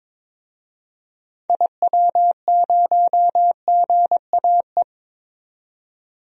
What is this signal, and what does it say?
Morse code, heard as a keyed tone, about 22 wpm, 700 Hz. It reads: IW0GAE
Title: DXCC-100 Call Signs - 6 Characters - 22wpm